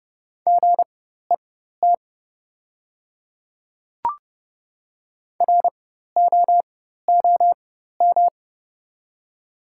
Morse code, heard as a keyed tone, about 30 wpm, spaced as for 14 wpm, 700 Hz. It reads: GET ROOM
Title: 4x Character Spacing - Top 300 Words - 30wpm